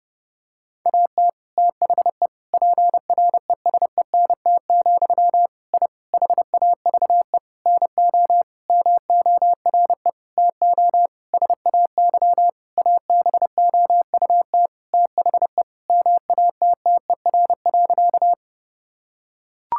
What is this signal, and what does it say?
Morse code, heard as a keyed tone, about 30 wpm, 700 Hz. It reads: AT THE PRESENT, I HAVE NO MORE TO SAY ABOUT THE MATTER.
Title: Sentences from Top 1000 Words - Review Part 5 - 30wpm